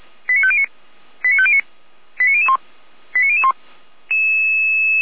Five-tone selective call (ZVEI), heard as quick runs of five tones, tones of about 70 ms. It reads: 89509 89509 89001 89001
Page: /455/SDR/ZVEI/